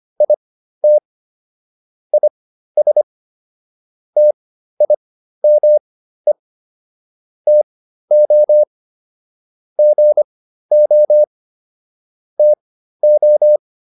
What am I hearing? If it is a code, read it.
Morse code, heard as a keyed tone, about 25 wpm, spaced as for 13 wpm, 600 Hz. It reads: IT IS TIME TO GO TO